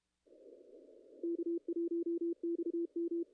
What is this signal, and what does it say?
Morse code, heard as a keyed tone, about 32 wpm, 340 Hz. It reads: K1XM
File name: contests / ss-cw-k5tr-op-n5rz -0000